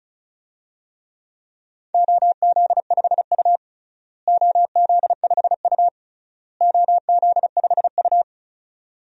Morse code, heard as a keyed tone, about 34 wpm, 700 Hz. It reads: OZ5U OZ5U OZ5U
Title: DXCC-100 Call Signs - 4 Characters - Repeated 3x - 35wpm